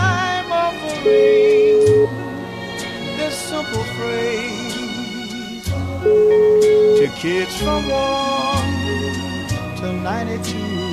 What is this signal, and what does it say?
Ringback tone (Europe, 425 Hz), with noise.